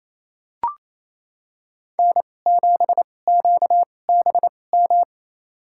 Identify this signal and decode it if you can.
Morse code, heard as a keyed tone, about 28 wpm, spaced as for 20 wpm, 700 Hz. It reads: N7QBM
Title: USA Call Signs - 5 Characters - 2x Character Spacing - 28wpm